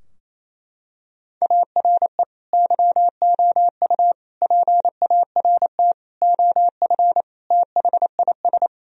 Morse code, heard as a keyed tone, about 28 wpm, 700 Hz. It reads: ARE YOU PART OF THIS
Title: Sentences from Top 1000 Words - Review Part 2 - 28wpm